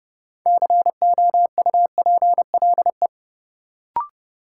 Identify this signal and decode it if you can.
Morse code, heard as a keyed tone, about 30 wpm, 700 Hz. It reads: COUPLE
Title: Sets of 2 Words - New Word Drill - 30wpm